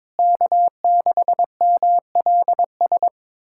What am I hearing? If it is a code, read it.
Morse code, heard as a keyed tone, about 22 wpm, 700 Hz. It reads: K6MLS
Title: USA Callsign DE Callsign 22wpm